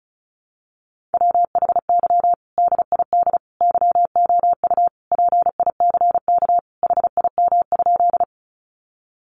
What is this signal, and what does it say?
Morse code, heard as a keyed tone, about 34 wpm, 700 Hz. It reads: WHY DID YOU PICK HIM?
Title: Sentences from Top 1000 Words 35wpm